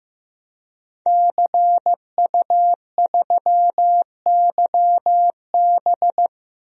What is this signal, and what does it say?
Morse code, heard as a keyed tone, about 15 wpm, 700 Hz. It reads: CU3YB
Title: DXCC-100 Call Signs - 5 Characters - 15wpm